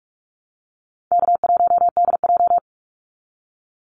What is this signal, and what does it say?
Morse code, heard as a keyed tone, about 44 wpm, 700 Hz. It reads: K1DJ
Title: Call Signs Repeated 3x - 45wpm